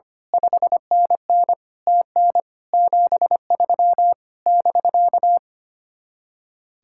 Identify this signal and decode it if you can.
Morse code, heard as a keyed tone, about 24 wpm, 700 Hz. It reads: 5NN TN 73 <BK>
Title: Speed-Racing Common QSO Phrases - 25wpm